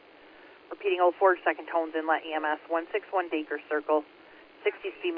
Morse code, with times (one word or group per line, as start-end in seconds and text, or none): none